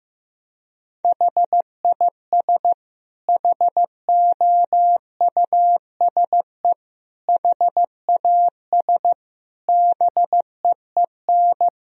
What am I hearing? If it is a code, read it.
Morse code, heard as a keyed tone, about 16 wpm, 700 Hz. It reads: HIS HOUSE HAS BEEN
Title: Sentences from Top 1000 Words - Review Part 3 - 15wpm